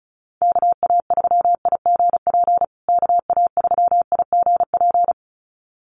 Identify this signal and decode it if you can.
Morse code, heard as a keyed tone, about 34 wpm, 700 Hz. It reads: KA3IGP KA3IGP
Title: USA Call Signs Repeated 2x 35wpm